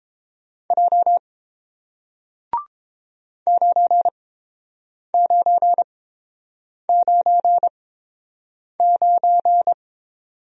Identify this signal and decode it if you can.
Morse code, heard as a keyed tone, about 30 wpm, spaced as for 7 wpm, 700 Hz. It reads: J 9999